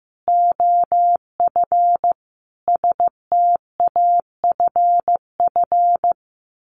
Morse code, heard as a keyed tone, about 15 wpm, 700 Hz. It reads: OF STAFF
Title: Sets of 5 Words Encore 15wpm